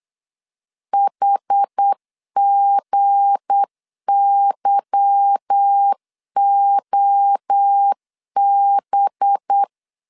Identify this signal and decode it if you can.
Morse code, heard as a keyed tone, about 8 wpm, 800 Hz. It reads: HGYOB
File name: LEZIONI / 8 YJ45 / 8 Y 8wpm 3245